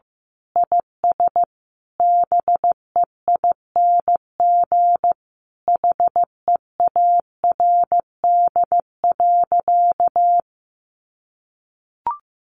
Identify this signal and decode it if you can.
Morse code, heard as a keyed tone, about 15 wpm, 700 Hz. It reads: IS BEING HEARD.
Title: Sentences from Top 1500 Words 15wpm